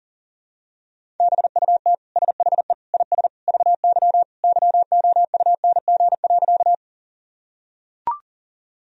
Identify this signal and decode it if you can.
Morse code, heard as a keyed tone, about 40 wpm, 700 Hz. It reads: BUT SHE IS VY YOUNG.
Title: Sentences from Top 400 Words and Abbreviations 40wpm